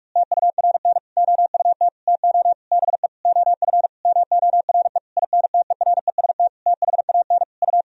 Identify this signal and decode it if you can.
Morse code, heard as a keyed tone, about 46 wpm, 700 Hz. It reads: TURN OUT TO BE OF MORE INTEREST THAN U
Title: Sentences from Top 400 Words and Abbreviations 45wpm